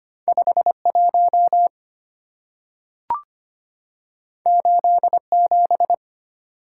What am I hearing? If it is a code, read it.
Morse code, heard as a keyed tone, about 25 wpm, 700 Hz. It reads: 5 1 8 7
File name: Two Letter-Number Combinations - 25wpm